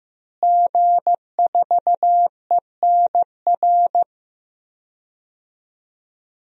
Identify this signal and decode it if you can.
Morse code, heard as a keyed tone, about 15 wpm, 700 Hz. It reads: G4ENR
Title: DXCC-100 Call Signs - 5 Characters - Repeated 3x - 15wpm